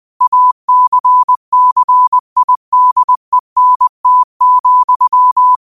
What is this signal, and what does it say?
Morse code, heard as a keyed tone, about 20 wpm, 1000 Hz. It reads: ACCIDENT,
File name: qotdfiles / 20wpm